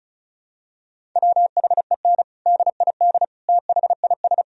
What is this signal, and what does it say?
Morse code, heard as a keyed tone, about 34 wpm, 700 Hz. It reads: WHEN DID THIS
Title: Sentences from Top 1000 Words - Review Part 1 - 35wpm